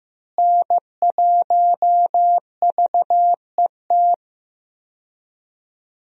Morse code, heard as a keyed tone, about 15 wpm, 700 Hz. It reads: N1VET
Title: USA Call Signs - 5 Characters - Repeated 3x - 15wpm